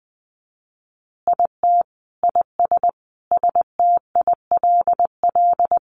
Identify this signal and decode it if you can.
Morse code, heard as a keyed tone, about 20 wpm, 700 Hz. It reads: IT IS STILL
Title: Easy Phrases 20wpm